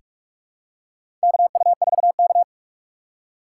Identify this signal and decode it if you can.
Morse code, heard as a keyed tone, about 46 wpm, 690 Hz. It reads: KU4K
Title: USA Call Signs - 4 Characters - 8x Character Spacing - 45wpm